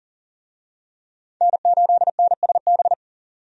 Morse code, heard as a keyed tone, about 40 wpm, 700 Hz. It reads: N8NSB
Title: USA Call Signs - 5 Characters - 40wpm